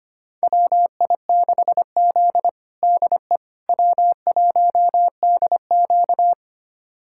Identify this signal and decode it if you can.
Morse code, heard as a keyed tone, about 25 wpm, 700 Hz. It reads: WI6Z DE W1DQ
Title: USA Callsign DE Callsign - 4 Characters - 25wpm